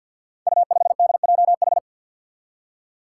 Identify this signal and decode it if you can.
Morse code, heard as a keyed tone, about 52 wpm, 700 Hz. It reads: U5DJH
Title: Five Challenging Letter-Number Combinations 50wpm